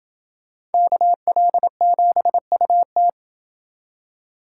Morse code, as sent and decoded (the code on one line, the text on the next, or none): -.- .-.. --... ..- -
KL7UT